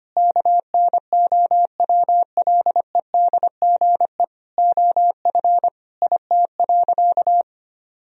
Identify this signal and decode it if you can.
Morse code, heard as a keyed tone, about 25 wpm, 700 Hz. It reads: KNOWLEDGE OF IT.
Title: Sentences from Top 1000 Words - Review Part 3 - 25wpm